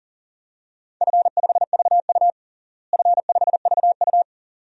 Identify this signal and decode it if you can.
Morse code, heard as a keyed tone, about 40 wpm, 700 Hz. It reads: F5VU F5VU
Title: DXCC-100 Call Signs - 4 Characters - Repeated 2x - 40wpm